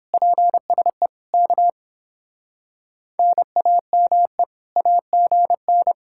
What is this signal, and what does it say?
Morse code, heard as a keyed tone, about 29 wpm, 700 Hz. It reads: PSE K NAME AGN